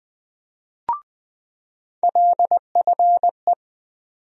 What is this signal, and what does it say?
Morse code, heard as a keyed tone, about 20 wpm, 700 Hz. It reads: LFE
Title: USA Call Sign Postfixes - 3 Letters 20wpm